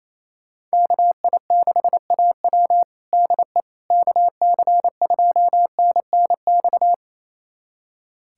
Morse code, heard as a keyed tone, about 28 wpm, 700 Hz. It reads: KI6AW DE KC2NNX
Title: USA Callsign DE Callsign 28wpm